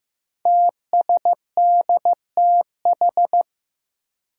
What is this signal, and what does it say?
Morse code, heard as a keyed tone, about 15 wpm, 700 Hz. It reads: TSDTH
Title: Five Challenging Letter-Number Combinations 15wpm